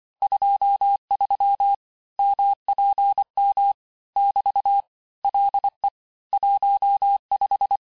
Morse code, heard as a keyed tone, about 24 wpm, 800 Hz. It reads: 23 MPM <BT> LE 15
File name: cours-du-lundi files / cours-1583--texte-clair-18-puis-23-mpm